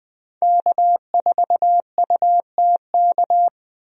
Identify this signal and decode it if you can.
Morse code, heard as a keyed tone, about 20 wpm, 700 Hz. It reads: K4UTK
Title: USA Call Signs - 5 Characters - 20wpm